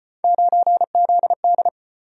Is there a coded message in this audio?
Yes: Morse code from 0.2 to 1.7 s, heard as a keyed tone, about 34 wpm, 700 Hz. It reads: 9ZD